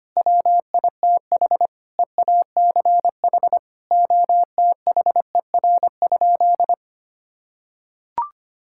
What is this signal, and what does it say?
Morse code, heard as a keyed tone, about 25 wpm, 700 Hz. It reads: WITH EACH OTHER?
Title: Sentences from Top 1000 Words - Review Part 6 - 25wpm